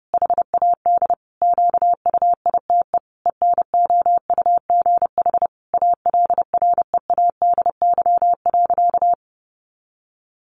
Morse code, heard as a keyed tone, about 30 wpm, 700 Hz. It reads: HAD QUITE ENOUGH ALREADY.
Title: Sentences from Top 700 Words 30wpm